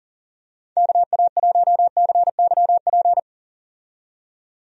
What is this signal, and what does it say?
Morse code, heard as a keyed tone, about 40 wpm, 700 Hz. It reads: KA1CYP
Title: USA Call Signs - 6 Characters - 40wpm